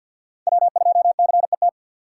Morse code, heard as a keyed tone, about 47 wpm, 700 Hz. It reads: W 2 C E T
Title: USA Call Signs - 6x Character Spacing - 50wpm